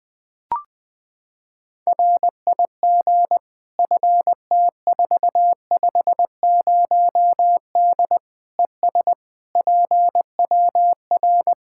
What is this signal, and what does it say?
Morse code, heard as a keyed tone, about 21 wpm, 700 Hz. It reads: RIG FT450D ES PWR
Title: Practice QSO - Exchange 2 - Info - 20wpm